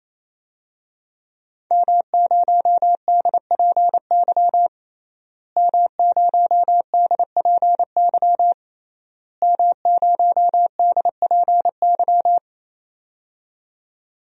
Morse code, heard as a keyed tone, about 27 wpm, 700 Hz. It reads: M0DPY M0DPY M0DPY
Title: DXCC-100 Call Signs - 5 Characters - Repeated 3x - 28wpm